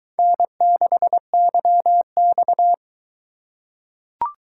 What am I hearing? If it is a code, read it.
Morse code, heard as a keyed tone, about 22 wpm, 700 Hz. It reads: N6YX